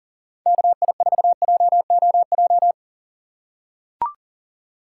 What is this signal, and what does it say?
Morse code, heard as a keyed tone, about 40 wpm, 700 Hz. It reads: KI4JOJ
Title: USA Call Signs - 6 Characters - 6x Character Spacing - 40wpm